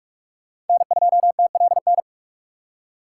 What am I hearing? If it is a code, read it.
Morse code, heard as a keyed tone, about 43 wpm, 700 Hz. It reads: N2TLN
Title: USA Call Signs - 5 Characters - 3x Character Spacing - 45wpm